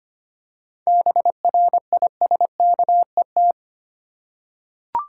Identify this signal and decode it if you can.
Morse code, heard as a keyed tone, about 25 wpm, 700 Hz. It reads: BRISKET
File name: practice / 25wpm / Difficult Words with Hints - 25wpm